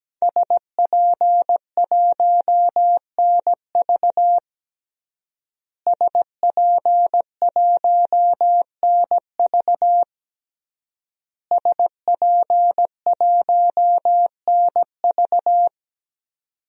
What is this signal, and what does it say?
Morse code, heard as a keyed tone, about 17 wpm, 700 Hz. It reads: SP1NV SP1NV SP1NV
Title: DXCC-100 Call Signs - 5 Characters - Repeated 3x - 17wpm